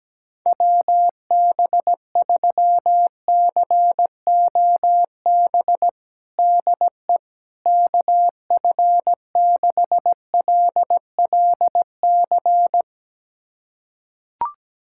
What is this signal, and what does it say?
Morse code, heard as a keyed tone, about 17 wpm, 700 Hz. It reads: WB3COB DE KF6LLC
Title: USA Callsign DE Callsign - 6 Characters - 17wpm